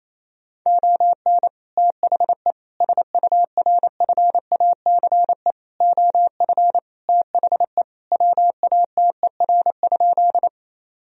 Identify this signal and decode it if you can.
Morse code, heard as a keyed tone, about 28 wpm, 700 Hz. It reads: ON THE SURFACE OF THE WATER?
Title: Sentences from Top 1000 Words - Review Part 3 - 28wpm